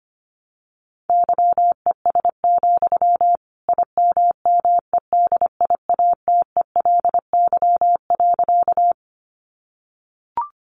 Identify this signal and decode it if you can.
Morse code, heard as a keyed tone, about 25 wpm, 700 Hz. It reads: YES, IMMEDIATELY.
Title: Sentences from Top 1100 Words 25wpm